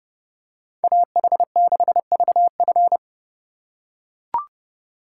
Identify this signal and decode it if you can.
Morse code, heard as a keyed tone, about 28 wpm, 700 Hz. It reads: AH6VF